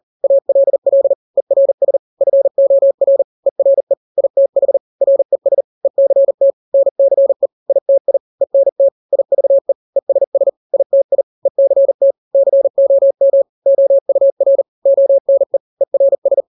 Morse code, heard as a keyed tone, about 40 wpm, 550 Hz. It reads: ALL ERS FOR ERE ITH RES ECT NCE ITI ENT IVE ESS ITI ECT COM OUR ONE ERS
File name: n-grams / 3-grams 40 35 WPM